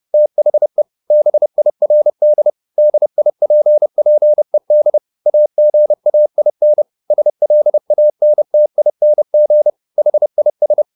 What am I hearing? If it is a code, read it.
Morse code, heard as a keyed tone, about 30 wpm, 600 Hz. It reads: THE BIRD DIPPED AGAIN SLANTING HIS
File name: the old man and the sea / the old man and the sea 1 30wpm